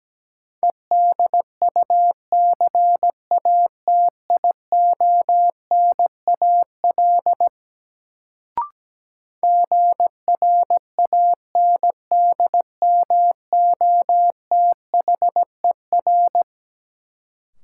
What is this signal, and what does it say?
Morse code, heard as a keyed tone, about 17 wpm, 700 Hz. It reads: EDUCATIONAL GRANDMOTHER
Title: Sentences from Top 1200 Words - New Word Drill - 17wpm